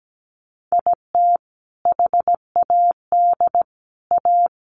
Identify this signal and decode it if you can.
Morse code, heard as a keyed tone, about 17 wpm, 700 Hz. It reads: IT HAD A
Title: Sentences from Top 1100 Words 17wpm